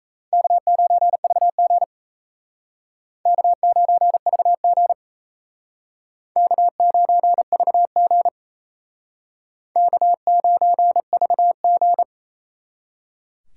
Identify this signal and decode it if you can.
Morse code, heard as a keyed tone, about 36 wpm, 700 Hz. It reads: K9VG K9VG K9VG K9VG